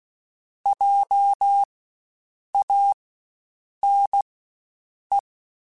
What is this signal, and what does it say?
Morse code, heard as a keyed tone, about 16 wpm, spaced as for 7 wpm, 800 Hz. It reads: JANE
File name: Spaced Alphas / Spaced Alphas mp3s / Js sp @15